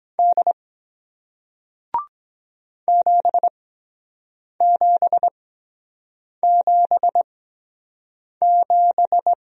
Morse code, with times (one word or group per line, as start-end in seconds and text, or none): 0.2-0.5 D
2.9-9.3 7777